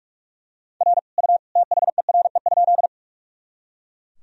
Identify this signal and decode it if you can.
Morse code, heard as a keyed tone, about 41 wpm, 700 Hz. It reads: R U THERE?
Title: Sentences from Top 100 Words and Abbreviations 45wpm